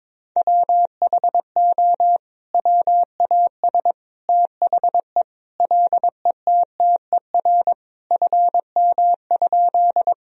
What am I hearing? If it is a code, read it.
Morse code, heard as a keyed tone, about 22 wpm, 700 Hz. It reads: WHO WAS THE LETTER FM?
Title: Sentences from Top 300 Words and Abbreviations 22wpm